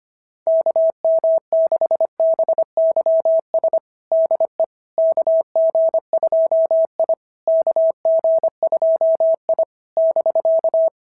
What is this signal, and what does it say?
Morse code, heard as a keyed tone, about 24 wpm, 650 Hz. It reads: KM6BYS DE KG2I KG2I <BK>
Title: SOTA QSO Practice 25wpm